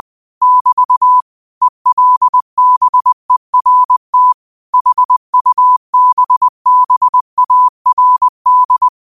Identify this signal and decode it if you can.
Morse code, heard as a keyed tone, about 20 wpm, 1000 Hz. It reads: <BT> ELBERT HUBBARD